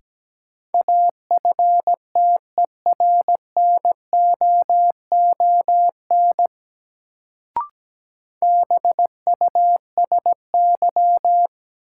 Morse code, heard as a keyed tone, about 17 wpm, 700 Hz. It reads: AFTERNOON BUSY
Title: Sentences from Top 1500 Words - New Word Drill Alt - 17wpm